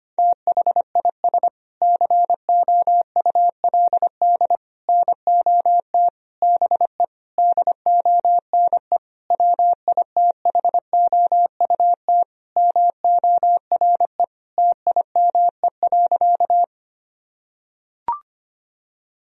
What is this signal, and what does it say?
Morse code, heard as a keyed tone, about 25 wpm, 700 Hz. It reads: THIS COULD NOT BE DONE WITHOUT MORE TIME.
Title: Sentences from Top 1000 Words - Review Part 3 - 25wpm